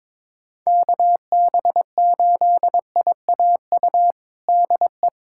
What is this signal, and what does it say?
Morse code, heard as a keyed tone, about 22 wpm, 700 Hz. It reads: KB8IAU DE
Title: USA Callsign DE Callsign - 6 Characters - 22wpm